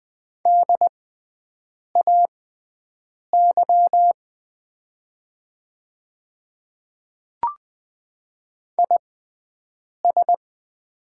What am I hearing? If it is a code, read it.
Morse code, heard as a keyed tone, about 20 wpm, spaced as for 7 wpm, 700 Hz. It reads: DAY IS
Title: 6x Character Spacing - Top 200 Words - 20wpm